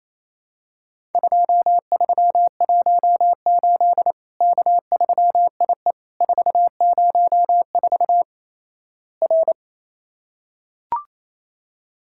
Morse code, heard as a keyed tone, about 28 wpm, 700 Hz. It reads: 2318 K3IE 404 R
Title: Worked All Europe DX Contest - QTC Exchange - 28wpm